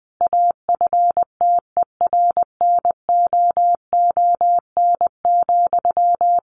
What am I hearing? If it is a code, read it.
Morse code, heard as a keyed tone, about 20 wpm, 700 Hz. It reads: AFTERNOON,
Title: Sentences from Top 1500 Words 20wpm